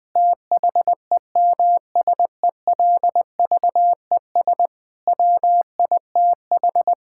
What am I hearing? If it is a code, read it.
Morse code, heard as a keyed tone, about 20 wpm, 700 Hz. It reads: THEMSELVES WITH
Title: Sentences from Top 1000 Words - Review Part 5 - 20wpm